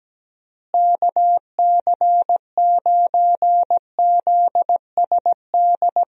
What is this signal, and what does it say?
Morse code, heard as a keyed tone, about 17 wpm, 700 Hz. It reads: KC9ZSD
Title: USA Call Signs - 6 Characters - Repeated 3x - 17wpm